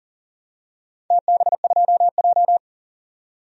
Morse code, heard as a keyed tone, about 40 wpm, 700 Hz. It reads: TB2J
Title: DXCC-100 Call Signs - 4 Characters - 40wpm